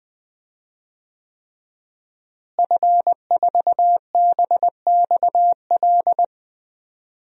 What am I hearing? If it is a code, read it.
Morse code, heard as a keyed tone, about 20 wpm, 700 Hz. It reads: F4BXL
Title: DXCC-100 Call Signs - 5 Characters - Repeated 3x - 20wpm